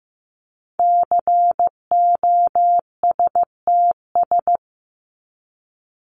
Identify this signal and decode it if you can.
Morse code, heard as a keyed tone, about 15 wpm, 700 Hz. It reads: COSTS
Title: ES:S Suffix - Top 100 Words with Base Words 15wpm